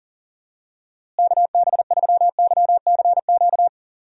Morse code, heard as a keyed tone, about 40 wpm, 700 Hz. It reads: KB3YCQ
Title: USA Call Signs - 6 Characters - Repeated 3x - 40wpm